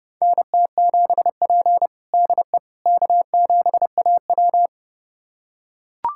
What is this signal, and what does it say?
Morse code, heard as a keyed tone, about 30 wpm, 700 Hz. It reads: NT7P DE K7AW
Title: USA Callsign DE Callsign - 4 Characters - 30wpm